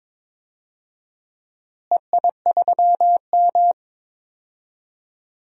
Morse code, heard as a keyed tone, about 22 wpm, 700 Hz. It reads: EI3M
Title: DXCC-100 Call Signs - 4 Characters - Repeated 3x - 22wpm